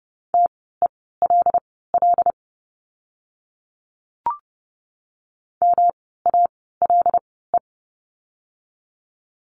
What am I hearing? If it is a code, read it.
Morse code, heard as a keyed tone, about 30 wpm, spaced as for 17 wpm, 700 Hz. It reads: TELL MALE
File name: course / 30:10wpm / 108 Lesson 18f - Letter m - Words 30wpm